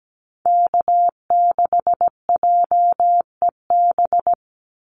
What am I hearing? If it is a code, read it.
Morse code, heard as a keyed tone, about 17 wpm, 700 Hz. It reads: K6JEB
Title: Call Signs - 17wpm